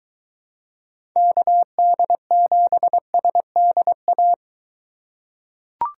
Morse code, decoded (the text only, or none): KD7SDA